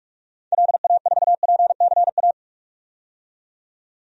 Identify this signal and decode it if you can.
Morse code, heard as a keyed tone, about 46 wpm, 700 Hz. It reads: LA4PCA
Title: DXCC-100 Call Signs - 6 Characters - 45wpm